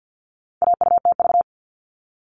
Morse code, heard as a keyed tone, about 50 wpm, 700 Hz. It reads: AUTV
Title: All Cut Numbers - 4-Digit 50wpm